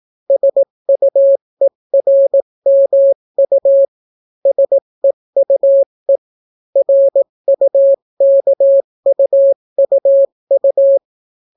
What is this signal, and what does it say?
Morse code, heard as a keyed tone, about 18 wpm, 550 Hz. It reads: SUERMU SEUE RUKUUU